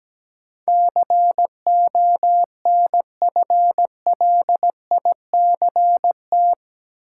Morse code, heard as a keyed tone, about 17 wpm, 700 Hz. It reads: CONFLICT
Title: Sentences from Top 1300 Words - New Word Drill - 17wpm